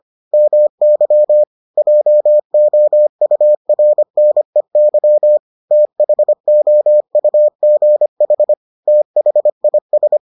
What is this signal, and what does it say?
Morse code, heard as a keyed tone, about 25 wpm, 600 Hz. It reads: MY JOURNEY THOUGH THIS